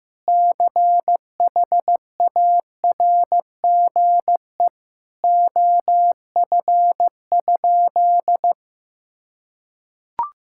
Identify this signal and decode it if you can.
Morse code, heard as a keyed tone, about 15 wpm, 700 Hz. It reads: CHARGE OF?
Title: Sentences from Top 1000 Words - Review Part 6 - 15wpm